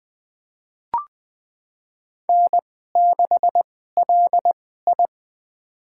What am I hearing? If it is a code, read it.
Morse code, heard as a keyed tone, about 20 wpm, spaced as for 14 wpm, 700 Hz. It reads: N6LI
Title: USA Call Signs - 4 Characters - 2x Character Spacing - 20wpm